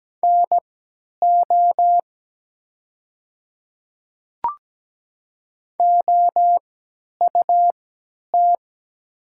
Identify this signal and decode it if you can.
Morse code, heard as a keyed tone, about 17 wpm, spaced as for 10 wpm, 700 Hz. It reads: NO OUT
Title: 3x Character Spacing - Top 100 Words - 17wpm